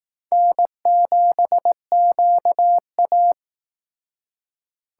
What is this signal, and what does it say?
Morse code, heard as a keyed tone, about 18 wpm, 700 Hz. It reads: N7QA
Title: Speed Racing - USA 1x2 Call Signs 15wpm